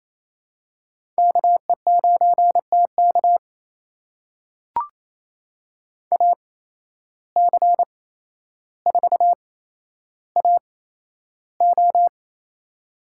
Morse code, heard as a keyed tone, about 28 wpm, spaced as for 12 wpm, 700 Hz. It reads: KE9TK AC4AO
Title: USA Call Signs - 5 Characters - 8x Character Spacing - 28wpm